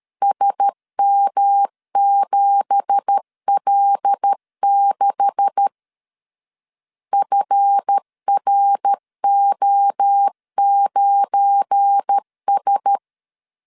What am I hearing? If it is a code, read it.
Morse code, heard as a keyed tone, about 13 wpm, 800 Hz. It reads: SM7L6 FRO9S
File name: CW / DET / 1 9 1500C S C 12wpm 3811